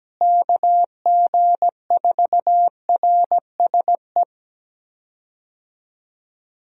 Morse code, heard as a keyed tone, about 17 wpm, 700 Hz. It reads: KG4RSE
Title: USA Call Signs - 6 Characters - Repeated 2x - 17wpm